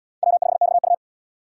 Morse code, heard as a keyed tone, about 70 wpm, 700 Hz. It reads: W5XU